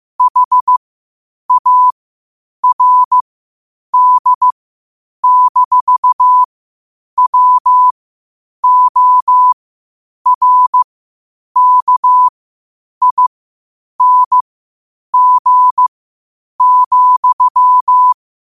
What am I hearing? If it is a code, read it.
Morse code, heard as a keyed tone, about 15 wpm, spaced as for 8 wpm, 1000 Hz. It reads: HARD-WORKING,
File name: qotdfiles / 8wpm